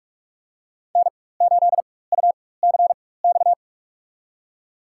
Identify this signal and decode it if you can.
Morse code, heard as a keyed tone, about 46 wpm, spaced as for 21 wpm, 700 Hz. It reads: N8UCX
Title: USA Call Signs - 4x Character Spacing - 45wpm